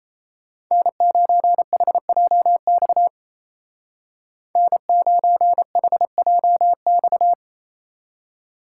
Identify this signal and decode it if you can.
Morse code, heard as a keyed tone, about 34 wpm, 700 Hz. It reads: N9HJX N9HJX
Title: Speed Racing - USA Call Signs 28wpm